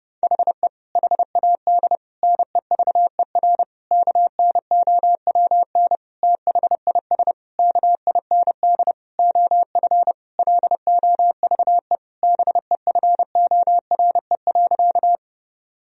Morse code, heard as a keyed tone, about 30 wpm, 700 Hz. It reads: HE HAD NEVER KNOWN THIS KIND OF LOVE BEFORE.
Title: Sentences from Top 1000 Words - Review Part 3 - 30wpm